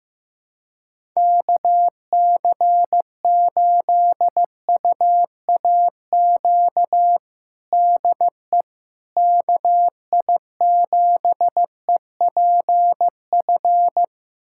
Morse code, heard as a keyed tone, about 15 wpm, 700 Hz. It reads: KC8UAQ DE KI7EPF
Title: USA Callsign DE Callsign - 6 Characters - 15wpm